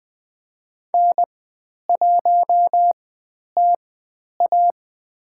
Morse code, heard as a keyed tone, about 20 wpm, spaced as for 10 wpm, 700 Hz. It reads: N1TA
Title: 109 Lesson 18g - Letter m - Call Signs 20wpm